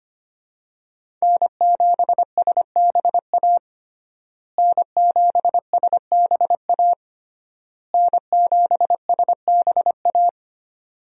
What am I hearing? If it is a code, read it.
Morse code, heard as a keyed tone, about 25 wpm, 700 Hz. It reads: N7SBA N7SBA N7SBA
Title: USA Call Signs - 5 Characters - Repeated 3x - 25wpm